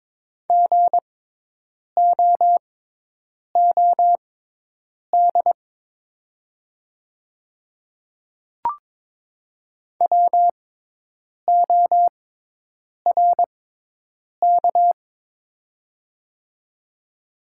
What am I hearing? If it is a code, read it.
Morse code, heard as a keyed tone, about 22 wpm, spaced as for 8 wpm, 700 Hz. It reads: GOOD WORK
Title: 6x Character Spacing - Top 200 Words - 22wpm